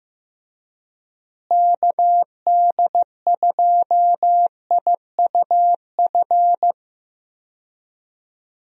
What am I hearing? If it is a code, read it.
Morse code, heard as a keyed tone, about 15 wpm, 700 Hz. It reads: KD2IUF
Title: USA Call Signs - 6 Characters - Repeated 2x - 15wpm